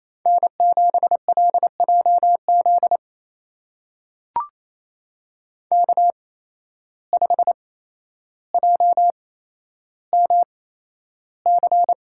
Morse code, heard as a keyed tone, about 28 wpm, spaced as for 12 wpm, 700 Hz. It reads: N7LJZ K5JMC
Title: USA Call Signs - 5 Characters - 8x Character Spacing - 28wpm